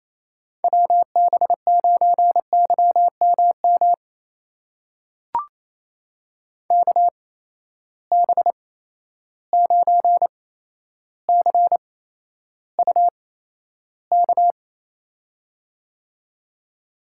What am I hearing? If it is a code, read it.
Morse code, heard as a keyed tone, about 28 wpm, spaced as for 12 wpm, 700 Hz. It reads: WB9YMM KB9CUK
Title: USA Call Signs - 6 Characters - 8x Character Spacing - 28wpm